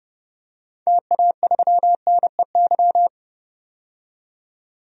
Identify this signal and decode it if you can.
Morse code, heard as a keyed tone, about 30 wpm, 700 Hz. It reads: TA3NEY
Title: DXCC-100 Call Signs - 6 Characters - 30wpm